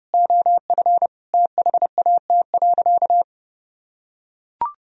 Morse code, heard as a keyed tone, about 30 wpm, 700 Hz. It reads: OF THAT.
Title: Sentences from Top 1000 Words - Review Part 2 - 30wpm